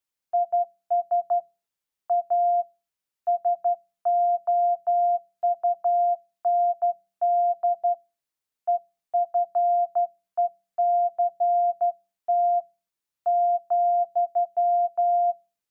Morse code, heard as a keyed tone, about 12 wpm, 700 Hz. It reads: IS A SOUND EFECT ,
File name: Morse Code01-2(Slow)